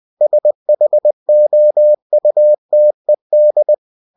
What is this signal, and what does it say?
Morse code, heard as a keyed tone, about 20 wpm, 600 Hz. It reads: SHOUTED
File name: cw-books / winnie-the-pooh / winnie-20wpm-page-0006